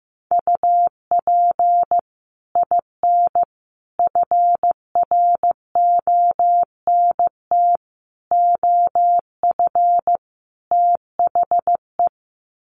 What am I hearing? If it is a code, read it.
Morse code, heard as a keyed tone, about 15 wpm, 700 Hz. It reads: UP IN FRONT OF THE
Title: Sets of 5 Words Encore2 15wpm